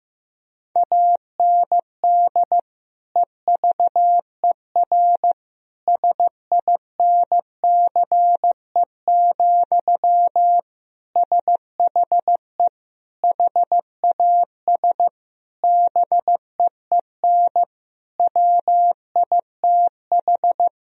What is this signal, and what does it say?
Morse code, heard as a keyed tone, about 15 wpm, 700 Hz. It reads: AND EVER SINCE, SHE HAS BEEN WITH
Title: Sentences from Top 1000 Words - Review Part 1 - 15wpm